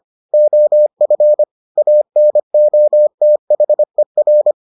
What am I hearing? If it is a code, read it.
Morse code, heard as a keyed tone, about 25 wpm, 600 Hz. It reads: OF ANOTHER